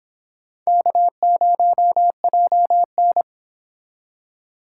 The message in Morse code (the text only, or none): K0JN